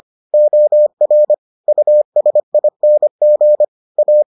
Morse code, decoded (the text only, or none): OR USING A